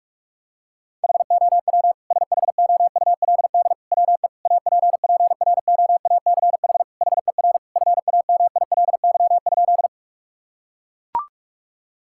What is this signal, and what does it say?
Morse code, heard as a keyed tone, about 41 wpm, 700 Hz. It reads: HOW SHOULD WE APPROACH HER FAMILY?
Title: Sentences from Top 1000 Words - Review Part 6 - 45wpm